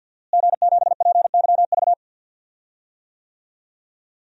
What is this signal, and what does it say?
Morse code, heard as a keyed tone, about 47 wpm, 700 Hz. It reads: G7PYV
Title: DXCC-100 Call Signs - 5 Characters - Repeated 2x - 50wpm